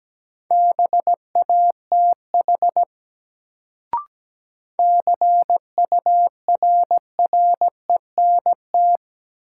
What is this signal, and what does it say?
Morse code, heard as a keyed tone, about 17 wpm, 700 Hz. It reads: BATH CURRENT
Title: Sentences from Top 1600 Words - New Word Drill Alt - 17wpm